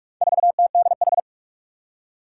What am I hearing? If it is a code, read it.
Morse code, heard as a keyed tone, about 41 wpm, 700 Hz. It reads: 4TDH
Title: Four Challenging Letter-Number Combinations 45wpm